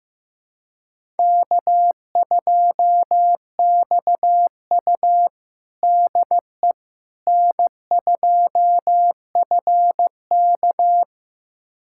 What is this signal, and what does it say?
Morse code, heard as a keyed tone, about 15 wpm, 700 Hz. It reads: K2XU DE N2FK
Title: USA Callsign DE Callsign - 4 Characters - 15wpm